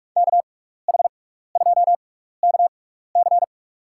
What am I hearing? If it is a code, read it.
Morse code, heard as a keyed tone, about 44 wpm, spaced as for 15 wpm, 700 Hz. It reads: KH2KC
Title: USA Call Signs - 5 Characters - 6x Character Spacing - 45wpm